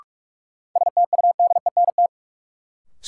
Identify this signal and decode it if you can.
Morse code, heard as a keyed tone, about 42 wpm, 700 Hz. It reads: STUDENT
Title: Sentences from Top 1000 Words - New Word Drill - 45wpm